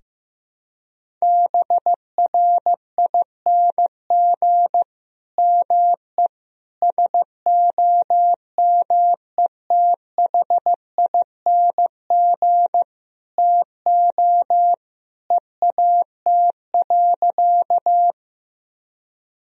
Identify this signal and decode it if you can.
Morse code, heard as a keyed tone, about 15 wpm, 700 Hz. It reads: BRING ME SOMETHING TO EAT.
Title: Sentences from Top 1000 Words - Review Part 3 - 15wpm